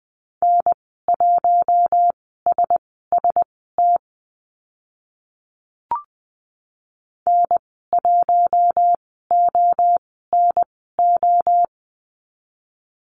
Morse code, heard as a keyed tone, about 20 wpm, spaced as for 15 wpm, 700 Hz. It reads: N1SST N1ONO
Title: 048 Lesson 9g - Number 4 - Call Signs 20wpm